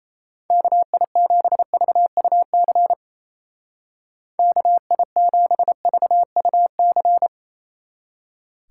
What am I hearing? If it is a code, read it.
Morse code, heard as a keyed tone, about 33 wpm, 700 Hz. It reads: KI7VUC KI7VUC
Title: Speed Racing - USA 2x3 Call Signs 28wpm